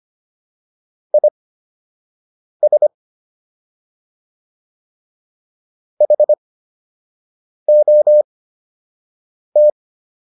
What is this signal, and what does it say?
Morse code, heard as a keyed tone, about 25 wpm, spaced as for 6 wpm, 600 Hz. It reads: IS HOT